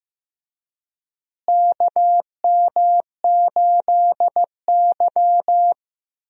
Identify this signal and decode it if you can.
Morse code, heard as a keyed tone, about 15 wpm, 700 Hz. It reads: KM8Y
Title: USA Call Signs - 4 Characters - Repeated 2x - 15wpm